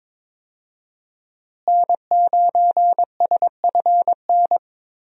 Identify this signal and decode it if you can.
Morse code, heard as a keyed tone, about 22 wpm, 700 Hz. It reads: N9SFN
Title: USA Call Signs - 5 Characters - Repeated 3x - 22wpm